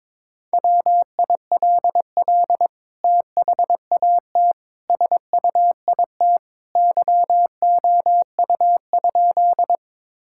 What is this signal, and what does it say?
Morse code, heard as a keyed tone, about 22 wpm, 700 Hz. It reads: WILL THAT SUIT YOU?
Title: Sentences from Top 1600 Words 22wpm